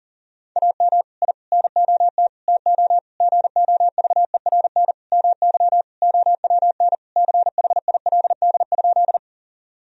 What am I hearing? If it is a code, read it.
Morse code, heard as a keyed tone, about 40 wpm, 700 Hz. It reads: AM I NOT TO GOVERN MY OWN CHILD?
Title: Sentences from Top 1000 Words - Review Part 2 - 40wpm